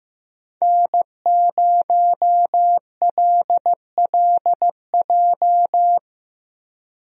Morse code, heard as a keyed tone, about 15 wpm, 700 Hz. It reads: N0LLJ
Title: USA Call Signs - 5 Characters - Repeated 2x - 15wpm